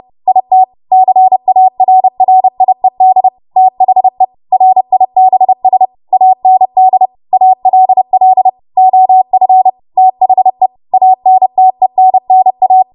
Morse code, heard as a keyed tone, about 30 wpm, 750 Hz. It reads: IT CARRIED THE RIBS AND ALL OF THE ANTENNA
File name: Morse / 30WPM